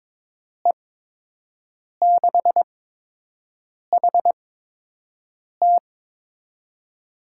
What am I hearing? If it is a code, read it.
Morse code, heard as a keyed tone, about 21 wpm, spaced as for 6 wpm, 700 Hz. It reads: E6HT
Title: USA Call Signs - 5 Characters - 8x Character Spacing - 22wpm